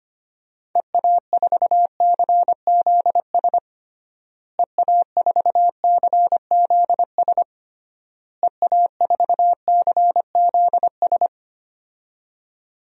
Morse code, heard as a keyed tone, about 25 wpm, 700 Hz. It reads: EA4CZS EA4CZS EA4CZS
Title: DXCC-100 Call Signs - 6 Characters - Repeated 3x - 25wpm